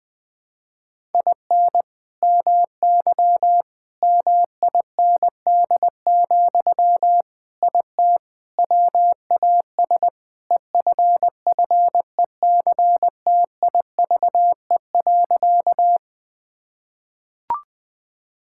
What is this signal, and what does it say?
Morse code, heard as a keyed tone, about 20 wpm, 700 Hz. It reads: IN MY MIND, IT WAS EFFECTIVE.
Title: Sentences from Top 1000 Words - Review Part 6 - 20wpm